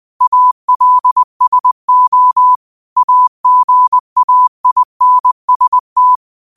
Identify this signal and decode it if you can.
Morse code, heard as a keyed tone, about 20 wpm, 1000 Hz. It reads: ALSO AGAINST
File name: qotdfiles / 20wpm